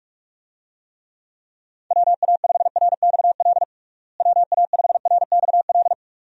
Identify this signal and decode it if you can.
Morse code, heard as a keyed tone, about 46 wpm, 700 Hz. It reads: WA5RXL WA5RXL
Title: USA Call Signs - 6 Characters - Repeated 3x - 45wpm